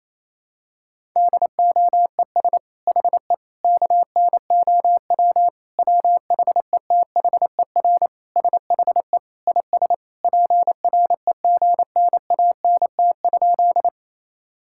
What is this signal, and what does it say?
Morse code, heard as a keyed tone, about 28 wpm, 700 Hz. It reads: DOES HE KNOW WHETHER SHE IS PREGNANT?
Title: Sentences from Top 1000 Words - Review Part 6 - 28wpm